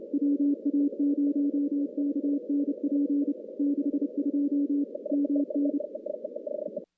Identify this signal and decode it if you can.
Morse code, heard as a keyed tone, about 30 wpm, 280 Hz. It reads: WA0KNP 62 MN
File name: contests / CW-SS-k5tr- -2100